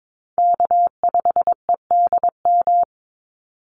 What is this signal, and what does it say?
Morse code, heard as a keyed tone, about 22 wpm, 700 Hz. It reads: K5EDM
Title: Call Signs Repeated 3x - 22wpm